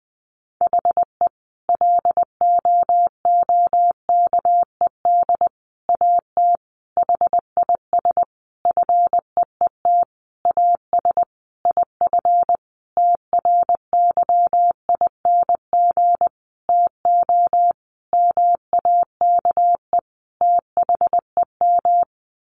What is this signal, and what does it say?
Morse code, heard as a keyed tone, about 20 wpm, 700 Hz. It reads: HE LOOKED AT HIS FEET AS IF TRYING TO MAKE THEM
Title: A Sound Of Thunder 20wpm